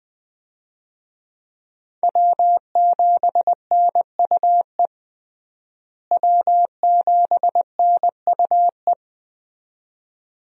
Morse code, heard as a keyed tone, about 20 wpm, 700 Hz. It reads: W7NUE W7NUE
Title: USA Call Signs - 5 Characters - Repeated 2x - 20wpm